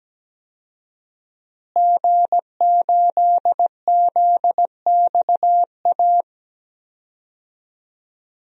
Morse code, heard as a keyed tone, about 17 wpm, 700 Hz. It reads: G8ZXA
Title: DXCC-100 Call Signs - 5 Characters - Repeated 3x - 17wpm